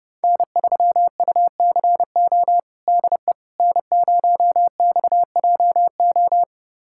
Morse code, heard as a keyed tone, about 30 wpm, 700 Hz. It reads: N3UCO DE N0XJO
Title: USA Callsign DE Callsign - 5 Characters - 30wpm